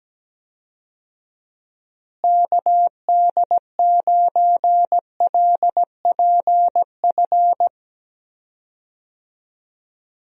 Morse code, heard as a keyed tone, about 17 wpm, 700 Hz. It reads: KD9LPF
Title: USA Call Signs - 6 Characters - Repeated 2x - 17wpm